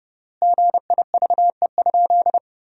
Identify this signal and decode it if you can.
Morse code, heard as a keyed tone, about 30 wpm, 700 Hz. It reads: GIVE?
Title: 3x Word-Spacing - Sentences from Top 200 Words - 30wpm